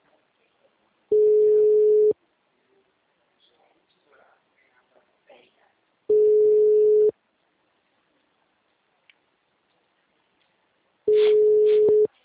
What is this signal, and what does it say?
Ringback tone (Europe, 425 Hz), with speech.